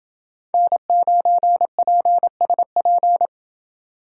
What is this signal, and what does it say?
Morse code, heard as a keyed tone, about 29 wpm, 700 Hz. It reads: N9PSP